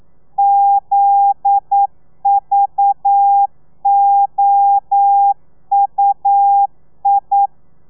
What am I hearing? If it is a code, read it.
Morse code, heard as a keyed tone, about 9 wpm, 800 Hz. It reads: ZVOUI
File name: telegrafia / i2tfj / Lez11 40